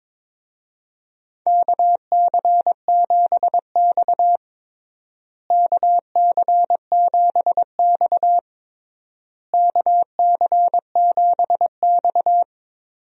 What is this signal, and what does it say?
Morse code, heard as a keyed tone, about 22 wpm, 700 Hz. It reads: KC7X KC7X KC7X
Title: USA Call Signs - 4 Characters - Repeated 3x - 22wpm